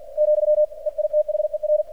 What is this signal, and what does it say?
Morse code, heard as a keyed tone, about 50 wpm, 600 Hz, with radio noise. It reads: , EITHER
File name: archived / CRSnet